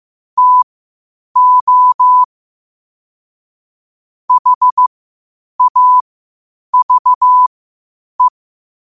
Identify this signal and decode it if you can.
Morse code, heard as a keyed tone, about 15 wpm, spaced as for 8 wpm, 1000 Hz. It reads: TO HAVE
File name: qotdfiles / 8wpm